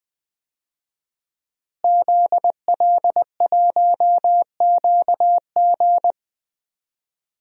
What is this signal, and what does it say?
Morse code, heard as a keyed tone, about 20 wpm, 700 Hz. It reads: ZL1QG
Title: DXCC-100 Call Signs - 5 Characters - Repeated 2x - 20wpm